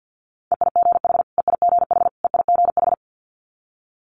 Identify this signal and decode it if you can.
Morse code, heard as a keyed tone, about 50 wpm, 690 Hz. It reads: EIGH EIGH EIGH
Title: Multi-letter Phonemes Repeated 3x 50wpm